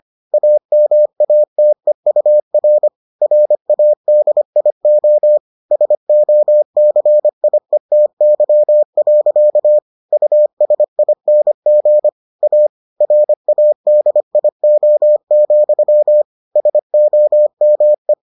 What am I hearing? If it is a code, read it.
Morse code, heard as a keyed tone, about 25 wpm, 600 Hz. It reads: AMATEUR RADIO SOCIETY. USING A RADIO, SOME